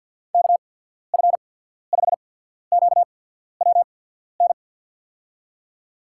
Morse code, heard as a keyed tone, about 53 wpm, spaced as for 14 wpm, 700 Hz. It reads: KF5QWN
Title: USA Call Signs - 6 Characters - 8x Character Spacing - 50wpm